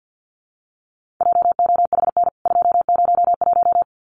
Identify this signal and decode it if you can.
Morse code, heard as a keyed tone, about 49 wpm, 700 Hz. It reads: JOHN 201
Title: CWT Exchange 50wpm